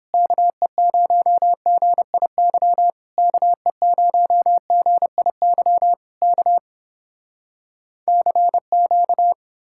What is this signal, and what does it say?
Morse code, heard as a keyed tone, about 30 wpm, 700 Hz. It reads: KE0GIY KE0GIY K CQ